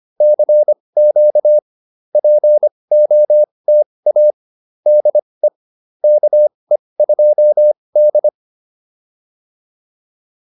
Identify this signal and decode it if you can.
Morse code, heard as a keyed tone, about 25 wpm, spaced as for 20 wpm, 600 Hz. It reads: CQ POTA DE KE2D